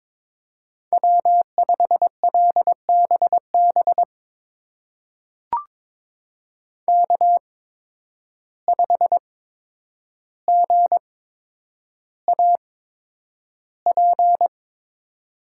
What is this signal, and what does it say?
Morse code, heard as a keyed tone, about 22 wpm, spaced as for 9 wpm, 700 Hz. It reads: W5LBB K5GAP
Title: USA Call Signs - 5 Characters - 8x Character Spacing - 22wpm